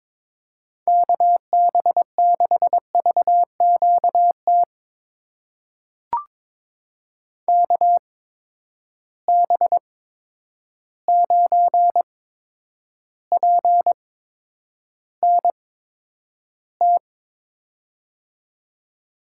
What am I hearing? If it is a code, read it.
Morse code, heard as a keyed tone, about 23 wpm, spaced as for 10 wpm, 700 Hz. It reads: KB6VQT KB9PNT
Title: USA Call Signs - 6 Characters - 8x Character Spacing - 22wpm